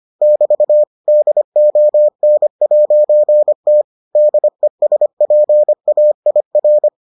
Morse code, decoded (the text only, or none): <BT> DON'T DESPAIR